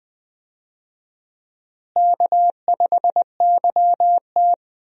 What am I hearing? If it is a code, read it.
Morse code, heard as a keyed tone, about 21 wpm, 700 Hz. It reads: K5YT
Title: USA Call Signs - 4 Characters - Repeated 2x - 20wpm